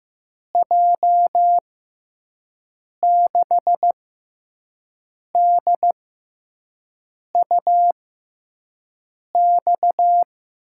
Morse code, heard as a keyed tone, about 15 wpm, spaced as for 5 wpm, 700 Hz. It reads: J6DUX
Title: USA Call Signs - 6 Characters - 6x Character Spacing - 15wpm